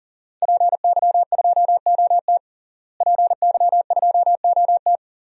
Morse code, heard as a keyed tone, about 40 wpm, 700 Hz. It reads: PY2OT PY2OT
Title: DXCC-100 Call Signs - 5 Characters - Repeated 3x - 40wpm